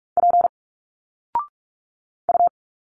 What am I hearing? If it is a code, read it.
Morse code, heard as a keyed tone, about 46 wpm, 700 Hz. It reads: P U
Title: Single Letters 45wpm